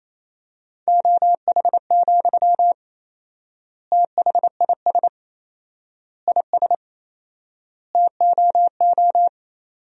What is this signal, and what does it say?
Morse code, heard as a keyed tone, about 28 wpm, 700 Hz. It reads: OH, THIS IS TOO
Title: 4x Word-Spacing - Sentences from Top 500 Words - 28wpm